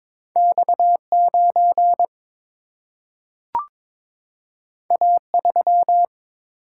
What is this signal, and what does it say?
Morse code, heard as a keyed tone, about 22 wpm, 700 Hz. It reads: X9 A3
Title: Single-Letter Single-Number 22wpm